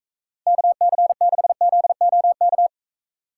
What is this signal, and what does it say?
Morse code, heard as a keyed tone, about 45 wpm, 700 Hz. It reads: KC6ZOK